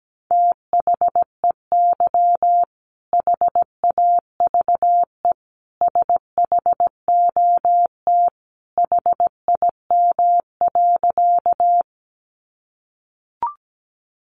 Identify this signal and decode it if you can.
Morse code, heard as a keyed tone, about 17 wpm, 700 Hz. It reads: THEY HAVE SHOT HIM.
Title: Sentences from Top 1100 Words 17wpm